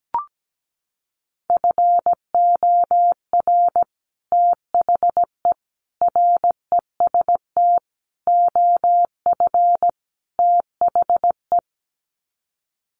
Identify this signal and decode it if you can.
Morse code, heard as a keyed tone, about 17 wpm, 700 Hz. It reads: FOR THE REST OF THE
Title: Sets of 5 Words 17wpm